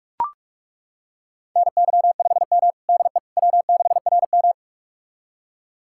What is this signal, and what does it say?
Morse code, heard as a keyed tone, about 41 wpm, 700 Hz. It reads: NY5M DE W6RM
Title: USA Callsign DE Callsign - 4 Characters - 45wpm